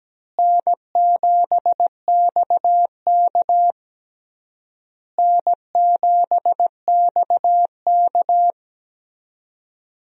Morse code, heard as a keyed tone, about 17 wpm, 700 Hz. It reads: N7XK N7XK
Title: USA Call Signs - 4 Characters - Repeated 3x - 17wpm